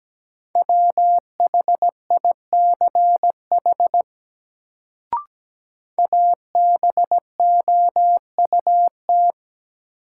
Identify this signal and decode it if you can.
Morse code, heard as a keyed tone, about 17 wpm, 700 Hz. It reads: WHICH ABOUT
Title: Sentences from Top 100 Words - New Word Drill - 17wpm